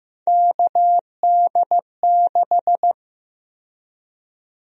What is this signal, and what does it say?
Morse code, heard as a keyed tone, about 15 wpm, 700 Hz. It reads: KD6
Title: USA Call Sign Prefixes - 15wpm